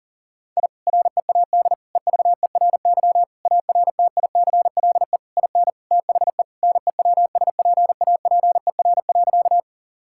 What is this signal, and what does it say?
Morse code, heard as a keyed tone, about 40 wpm, 700 Hz. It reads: I READ EVERY ARTICLE IN THE NEWSPAPER.
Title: Sentences from Top 1000 Words - Review Part 5 - 40wpm